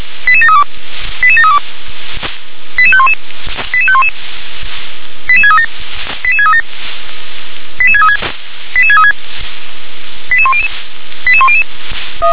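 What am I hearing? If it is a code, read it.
Five-tone selective call (ZVEI), heard as quick runs of five tones, tones of about 80 ms. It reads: 88732 88732 88511 88511 88637 88637 88636 88636 88100 88100
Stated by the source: ZVEI five-tone selective call 88513 received with an SDR.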